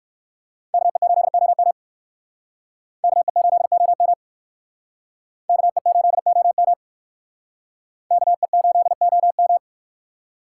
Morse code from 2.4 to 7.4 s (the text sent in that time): KE8OM KE8OM